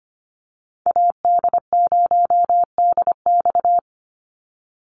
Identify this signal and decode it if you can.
Morse code, heard as a keyed tone, about 25 wpm, 700 Hz. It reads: AD0DX
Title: Call Signs - 25wpm